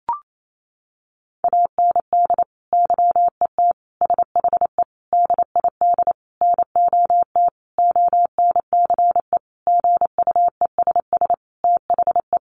Morse code, heard as a keyed tone, about 28 wpm, 700 Hz. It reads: AND YET SHE DID NOT ONCE GUESS THE
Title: Sentences from Top 1500 Words 28wpm